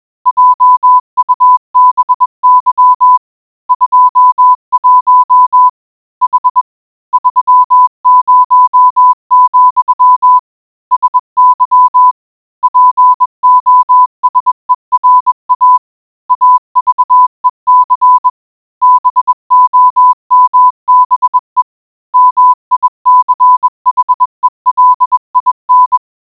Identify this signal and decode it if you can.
Morse code, heard as a keyed tone, about 20 wpm, 1000 Hz. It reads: JUBY 21 H 30, SY POSERA AVEC BOMBE MICHELIN